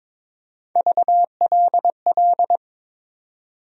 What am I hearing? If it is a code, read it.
Morse code, heard as a keyed tone, about 22 wpm, 700 Hz. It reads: VLL
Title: USA Call Sign Postfixes - 3 Letters 22wpm